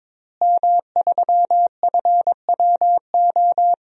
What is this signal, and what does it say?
Morse code, heard as a keyed tone, about 22 wpm, 700 Hz. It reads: M3FWO
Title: DXCC-100 Call Signs - 5 Characters - Repeated 3x - 22wpm